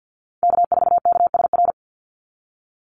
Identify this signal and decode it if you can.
Morse code, heard as a keyed tone, about 48 wpm, 700 Hz. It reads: K4KSR
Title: Call Signs Repeated 2x - 50wpm